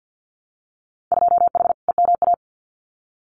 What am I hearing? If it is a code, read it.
Morse code, heard as a keyed tone, about 50 wpm, 700 Hz. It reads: 2H EMA
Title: Winter Field Day Exchanges 50wpm